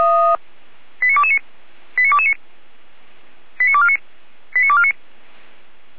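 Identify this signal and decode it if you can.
Five-tone selective call (ZVEI), heard as quick runs of five tones, tones of about 70 ms. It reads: 89229 89229 89249 89249
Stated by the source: ZVEI five-tone selective call received by SDR.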